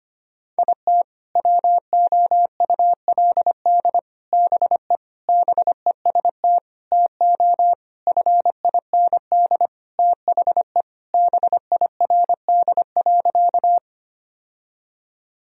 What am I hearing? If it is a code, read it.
Morse code, heard as a keyed tone, about 25 wpm, 700 Hz. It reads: IT WOULD BE BEST TO FIND THE BIRD.
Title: Sentences from Top 1000 Words - Review Part 2 - 25wpm